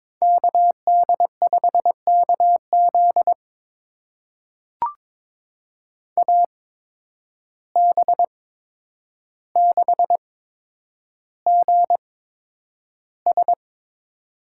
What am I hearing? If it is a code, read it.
Morse code, heard as a keyed tone, about 22 wpm, spaced as for 9 wpm, 700 Hz. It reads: KD5KZ AB6GS
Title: USA Call Signs - 5 Characters - 8x Character Spacing - 22wpm